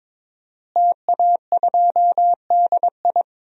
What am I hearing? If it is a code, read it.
Morse code, heard as a keyed tone, about 22 wpm, 700 Hz. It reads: TA2DI
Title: DXCC-100 Call Signs - 5 Characters - Repeated 2x - 22wpm